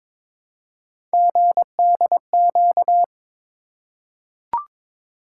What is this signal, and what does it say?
Morse code, heard as a keyed tone, about 22 wpm, 700 Hz. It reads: GDQ
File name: practice / 22wpm / Three Letter-Number Combinations - 22wpm